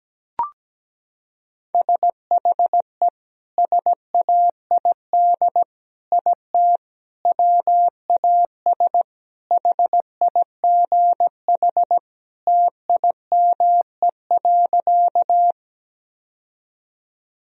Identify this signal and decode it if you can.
Morse code, heard as a keyed tone, about 17 wpm, 700 Hz. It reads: SHE SAID IT WAS HIGH TIME.
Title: Sentences from Top 1000 Words - Review Part 1 - 17wpm